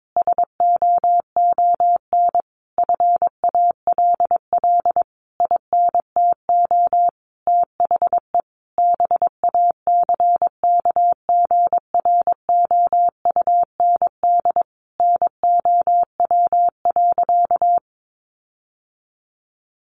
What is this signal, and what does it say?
Morse code, heard as a keyed tone, about 22 wpm, 700 Hz. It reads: SOON FALL INTO THE BACKGROUND NOW.
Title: Sentences from Top 1300 Words 22wpm